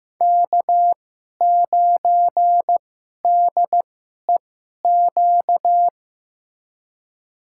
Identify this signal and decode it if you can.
Morse code, heard as a keyed tone, about 15 wpm, spaced as for 11 wpm, 700 Hz. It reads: K9DEQ
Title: USA Call Signs - 5 Characters - 2x Character Spacing - 15wpm